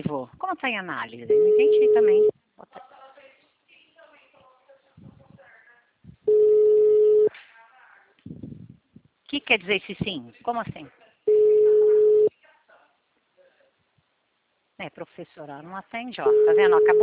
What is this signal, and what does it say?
Ringback tone (Europe, 425 Hz), with speech.